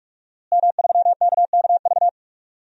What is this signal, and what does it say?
Morse code, heard as a keyed tone, about 46 wpm, 700 Hz. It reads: M3KKV
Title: DXCC-100 Call Signs - 5 Characters - Repeated 3x - 45wpm